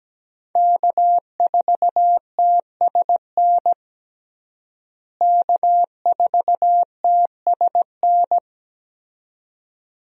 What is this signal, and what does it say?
Morse code, heard as a keyed tone, about 17 wpm, 700 Hz. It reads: K4TSN K4TSN
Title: USA Call Signs - 5 Characters - Repeated 2x - 17wpm